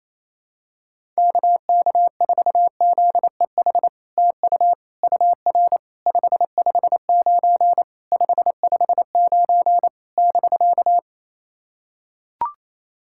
Morse code, heard as a keyed tone, about 28 wpm, 700 Hz. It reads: KK4ZEH TU UR 559 559 <BK>
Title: SOTA QSO Practice 28wpm